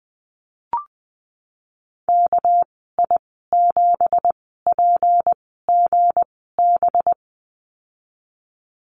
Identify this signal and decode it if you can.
Morse code, heard as a keyed tone, about 20 wpm, spaced as for 14 wpm, 700 Hz. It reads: KI7PGB
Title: 219 Lesson 35g - Number 0 - Call Signs 20wpm